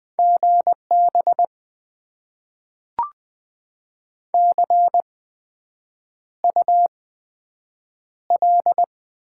Morse code, heard as a keyed tone, about 20 wpm, spaced as for 7 wpm, 700 Hz. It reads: GB CUL
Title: Top 90 QSO Elements - 8x Character Spacing - 20wpm